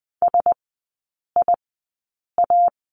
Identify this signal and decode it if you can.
Morse code, heard as a keyed tone, about 20 wpm, spaced as for 8 wpm, 700 Hz. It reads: SIA
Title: 047 Lesson 9e - Number 4 - Three Characters 20wpm